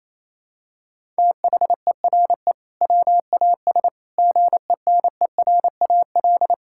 Morse code, heard as a keyed tone, about 28 wpm, 700 Hz. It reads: THERE WAS GENERAL
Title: Sentences from Top 1600 Words 28wpm